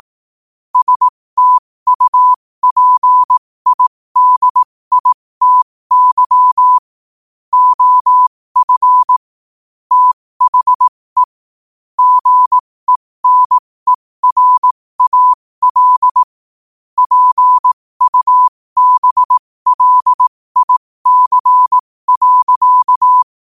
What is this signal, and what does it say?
Morse code, heard as a keyed tone, about 18 wpm, spaced as for 15 wpm, 1000 Hz. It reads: STUPIDITY OF THE GENERAL PUBLIC.
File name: qotdfiles / 15wpm